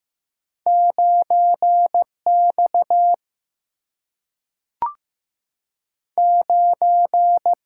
Morse code, heard as a keyed tone, about 15 wpm, 700 Hz. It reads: 9X 9
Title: Single-Letter Single-Number 15wpm